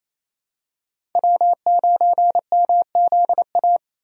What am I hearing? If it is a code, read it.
Morse code, heard as a keyed tone, about 28 wpm, 700 Hz. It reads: W9MZA
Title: USA Call Signs - 5 Characters - Repeated 2x - 28wpm